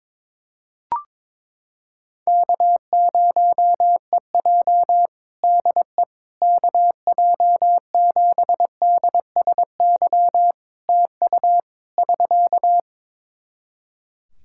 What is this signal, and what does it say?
Morse code, heard as a keyed tone, about 22 wpm, 700 Hz. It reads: K0EJ DE KJ7DSY TU <SK>
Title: Practice QSO - Exchange 4 - Ending - 22wpm